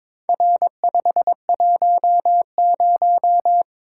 Morse code, heard as a keyed tone, about 23 wpm, 700 Hz. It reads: R510
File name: practice / 22wpm / Four Letter-Number Combinations - 22wpm